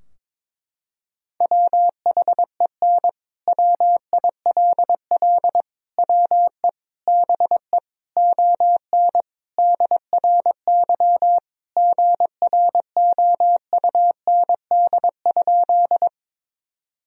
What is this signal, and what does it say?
Morse code, heard as a keyed tone, about 22 wpm, 700 Hz. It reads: WHEN WILL WE BE ON DRY GROUND?
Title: Sentences from Top 1000 Words - Review Part 3 - 22wpm